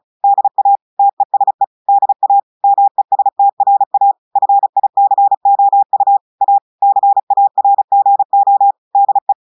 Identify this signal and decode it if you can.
Morse code, heard as a keyed tone, about 36 wpm, 800 Hz. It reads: DA TESE DA MESTRA FICOU A CARGO DE
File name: audios / portugais / portugais texte en clair 1 vitesse 35 mots minutes